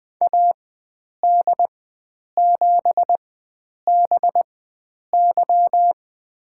Morse code, heard as a keyed tone, about 20 wpm, spaced as for 9 wpm, 700 Hz. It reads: AD7BY
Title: USA Call Signs - 4x Character Spacing - 20wpm